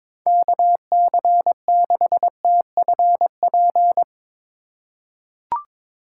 Morse code, heard as a keyed tone, about 22 wpm, 700 Hz. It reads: KC6TFP
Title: USA Call Signs - 6 Characters - 3x Character Spacing - 22wpm